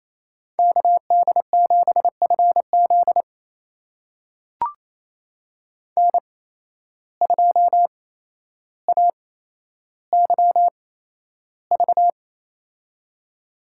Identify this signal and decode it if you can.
Morse code, heard as a keyed tone, about 28 wpm, spaced as for 12 wpm, 700 Hz. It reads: KD7FZ N2AYV
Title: USA Call Signs - 5 Characters - 8x Character Spacing - 28wpm